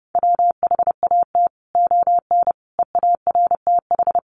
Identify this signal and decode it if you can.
Morse code, heard as a keyed tone, about 30 wpm, 700 Hz. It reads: WHAT ON EARTH
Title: Sentences from Top 700 Words 30wpm